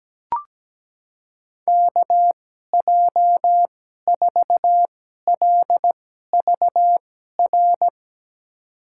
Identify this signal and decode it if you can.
Morse code, heard as a keyed tone, about 17 wpm, spaced as for 12 wpm, 700 Hz. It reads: KJ4LVR
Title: USA Call Signs - 6 Characters - 2x Character Spacing - 17wpm